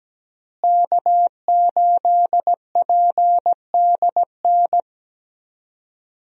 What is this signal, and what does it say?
Morse code, heard as a keyed tone, about 17 wpm, 700 Hz. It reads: K8PDN
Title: USA Call Signs - 5 Characters - Repeated 2x - 17wpm